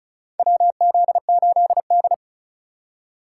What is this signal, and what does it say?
Morse code, heard as a keyed tone, about 36 wpm, 700 Hz. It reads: WZ8D
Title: USA Call Signs - 4 Characters - 3x Character Spacing - 35wpm